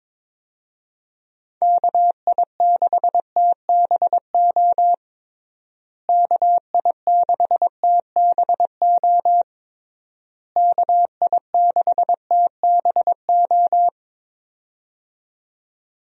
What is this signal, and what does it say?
Morse code, heard as a keyed tone, about 22 wpm, 700 Hz. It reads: KI6TBO KI6TBO KI6TBO
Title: USA Call Signs - 6 Characters - Repeated 3x - 22wpm